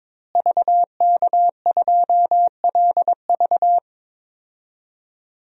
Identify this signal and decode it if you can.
Morse code, heard as a keyed tone, about 22 wpm, 700 Hz. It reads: VK2LV
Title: DXCC-100 Call Signs - 5 Characters - Repeated 2x - 22wpm